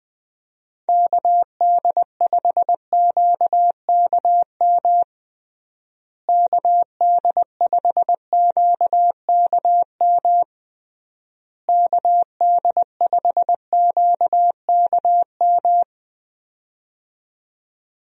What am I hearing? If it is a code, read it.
Morse code, heard as a keyed tone, about 20 wpm, 700 Hz. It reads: KD5QKM KD5QKM KD5QKM
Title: USA Call Signs - 6 Characters - Repeated 3x - 20wpm